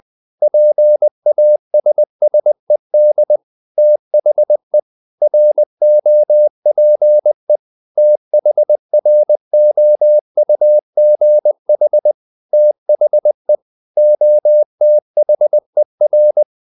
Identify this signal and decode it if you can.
Morse code, heard as a keyed tone, about 20 wpm, 600 Hz. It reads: PASSED THE ROPE THROUGH THE OTHER
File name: the old man and the sea part 2 20wpm